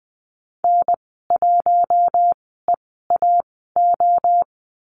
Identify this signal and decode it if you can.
Morse code, heard as a keyed tone, about 20 wpm, spaced as for 15 wpm, 700 Hz. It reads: N1EAO
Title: 048 Lesson 9g - Number 4 - Call Signs 20wpm